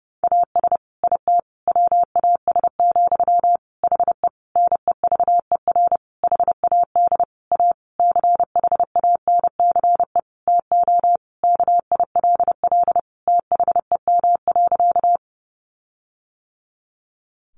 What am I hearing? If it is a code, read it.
Morse code, heard as a keyed tone, about 30 wpm, 700 Hz. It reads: AS IT WAS, HE NEVER HAD A CHANCE TO KILL THEM.
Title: Sentences from Top 1400 Words 30wpm